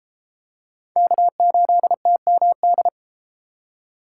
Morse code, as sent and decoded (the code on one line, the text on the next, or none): -.- ---.. - -- -..
K8TMD